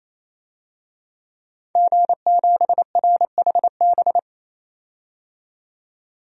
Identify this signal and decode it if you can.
Morse code, heard as a keyed tone, about 28 wpm, 700 Hz. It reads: G7RHB
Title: DXCC-100 Call Signs - 5 Characters - 28wpm